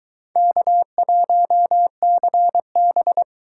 Morse code, heard as a keyed tone, about 23 wpm, 700 Hz. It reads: K1CB